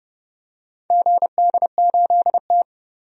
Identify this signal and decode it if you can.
Morse code, heard as a keyed tone, about 30 wpm, 700 Hz. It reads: GD8T
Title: DXCC-100 Call Signs - 4 Characters - 30wpm